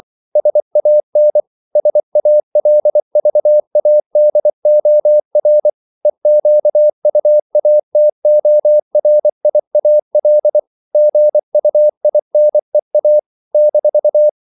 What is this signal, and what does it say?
Morse code, heard as a keyed tone, about 24 wpm, 600 Hz. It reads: SAN SALVADOR EQUATORIAL GUINEA -
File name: CW / Practice / NationalCapitals24